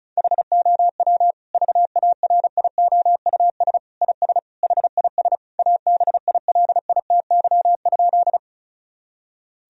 Morse code, heard as a keyed tone, about 36 wpm, 700 Hz. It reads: HOW VARIOUS IS HIS ABILITY?
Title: Sentences from Top 1000 Words - Review Part 4 - 35wpm